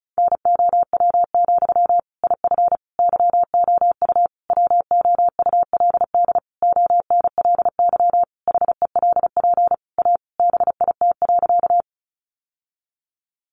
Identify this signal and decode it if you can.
Morse code, heard as a keyed tone, about 34 wpm, 700 Hz. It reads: NOW, IF YOU WOULD ONLY HELP A BIT.
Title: Sentences from Top 900 Words 35wpm